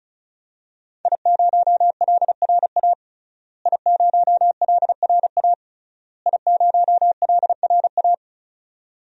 Morse code, heard as a keyed tone, about 34 wpm, 700 Hz. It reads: I0LRA I0LRA I0LRA
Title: DXCC-100 Call Signs - 5 Characters - Repeated 3x - 35wpm